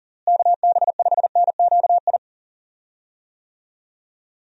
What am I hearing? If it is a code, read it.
Morse code, heard as a keyed tone, about 40 wpm, 700 Hz. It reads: KB5NQI
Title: USA Call Signs - 6 Characters - Repeated 2x - 40wpm